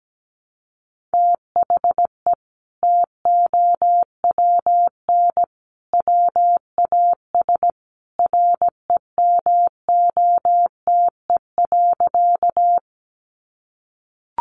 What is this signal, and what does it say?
Morse code, heard as a keyed tone, about 17 wpm, 700 Hz. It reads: THE TOWN WAS REMOTE.
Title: Sentences from Top 1300 Words 17wpm